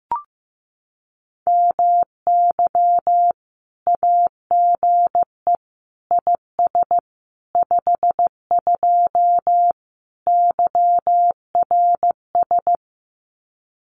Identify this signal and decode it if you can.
Morse code, heard as a keyed tone, about 15 wpm, 700 Hz. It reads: MY AGE IS 52 YRS
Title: Common QSO Phrases Encore 15wpm